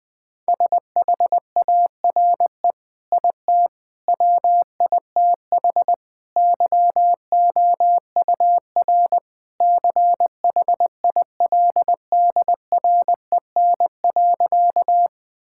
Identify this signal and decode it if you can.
Morse code, heard as a keyed tone, about 20 wpm, 700 Hz. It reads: SHARE IT WITH YOUR CHILDREN.
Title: Sentences from Top 1000 Words - Review Part 6 - 20wpm